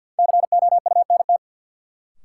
Morse code, heard as a keyed tone, about 53 wpm, 700 Hz. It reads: COUNT
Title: Sentences from Top 1500 Words - New Word Drill - 50wpm